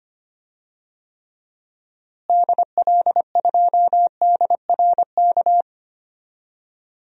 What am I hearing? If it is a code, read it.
Morse code, heard as a keyed tone, about 25 wpm, 700 Hz. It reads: DL2DRK
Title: DXCC-100 Call Signs - 6 Characters - Repeated 2x - 25wpm